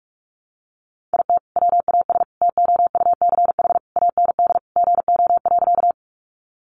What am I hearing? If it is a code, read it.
Morse code, heard as a keyed tone, about 44 wpm, 700 Hz. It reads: IT WAS TOUCH AND GO.
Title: Sentences from Top 1200 Words 45wpm